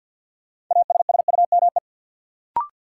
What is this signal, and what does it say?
Morse code, heard as a keyed tone, about 51 wpm, 700 Hz. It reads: ASSUME
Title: Sentences from Top 1500 Words - New Word Drill - 50wpm